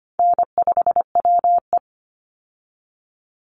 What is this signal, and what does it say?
Morse code, heard as a keyed tone, about 25 wpm, 700 Hz. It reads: N5WE
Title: Easy Call Signs - 25wpm